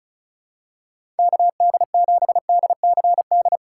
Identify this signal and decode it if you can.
Morse code, heard as a keyed tone, about 36 wpm, 700 Hz. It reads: KD7DCD
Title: USA Call Signs - 2x Character Spacing - 35wpm